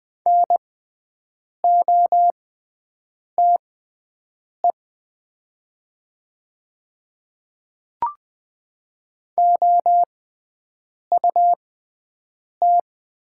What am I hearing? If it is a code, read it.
Morse code, heard as a keyed tone, about 20 wpm, spaced as for 7 wpm, 700 Hz. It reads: NOTE OUT
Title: 6x Character Spacing - Top 500 Words - 20wpm